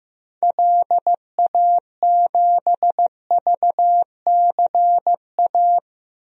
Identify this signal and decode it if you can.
Morse code, heard as a keyed tone, about 15 wpm, 700 Hz. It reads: LA7VCA
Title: DXCC-100 Call Signs - 6 Characters - 15wpm